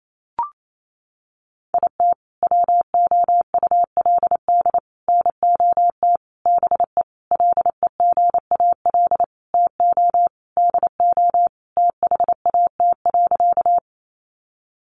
Morse code, heard as a keyed tone, about 28 wpm, 700 Hz. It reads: IT WOULD NOT BE LEGAL TO DO THAT.
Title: Sentences from Top 1200 Words 28wpm